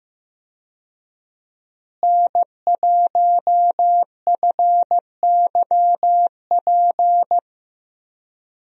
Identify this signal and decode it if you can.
Morse code, heard as a keyed tone, about 15 wpm, 700 Hz. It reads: N1FYP
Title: USA Call Signs - 5 Characters - Repeated 2x - 15wpm